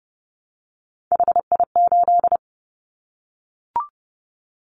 Morse code, heard as a keyed tone, about 30 wpm, 700 Hz. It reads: HI8
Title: Callsign Prefixes for Top 100 DXCC - 30wpm